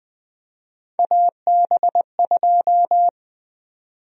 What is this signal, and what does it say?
Morse code, heard as a keyed tone, about 20 wpm, 700 Hz. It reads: AB2
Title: USA Call Sign Prefixes - 20wpm